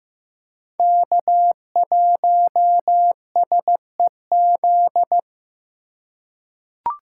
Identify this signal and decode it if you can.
Morse code, heard as a keyed tone, about 15 wpm, 700 Hz. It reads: K1SEZ
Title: USA Call Signs - 5 Characters - 4x Character Spacing - 15wpm